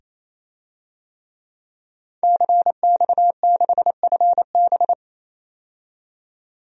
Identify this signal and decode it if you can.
Morse code, heard as a keyed tone, about 28 wpm, 700 Hz. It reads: CX6FB
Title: DXCC-100 Call Signs - 5 Characters - 28wpm